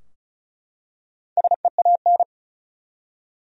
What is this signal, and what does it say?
Morse code, heard as a keyed tone, about 34 wpm, 700 Hz. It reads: SEAN
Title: USA Common Names - Max 4 Letters - 35wpm